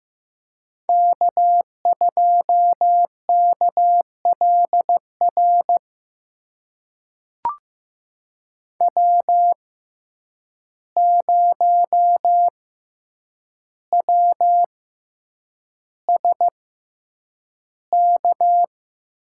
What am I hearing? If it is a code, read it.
Morse code, heard as a keyed tone, about 15 wpm, spaced as for 8 wpm, 700 Hz. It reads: K2KLR W0WSK
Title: USA Call Signs - 5 Characters - 6x Character Spacing - 15wpm